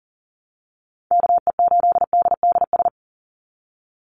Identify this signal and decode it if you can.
Morse code, heard as a keyed tone, about 40 wpm, 700 Hz. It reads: KE8DDS
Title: USA Call Signs 40wpm